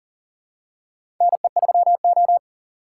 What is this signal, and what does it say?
Morse code, heard as a keyed tone, about 40 wpm, 700 Hz. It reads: NE3O
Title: USA Call Signs - 4 Characters - 3x Character Spacing - 40wpm